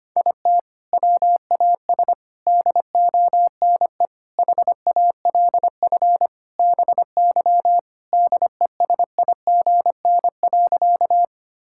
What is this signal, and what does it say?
Morse code, heard as a keyed tone, about 25 wpm, 700 Hz. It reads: IT WAS DONE HALF BY DESIGN.
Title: Sentences from Top 1000 Words - Review Part 5 - 25wpm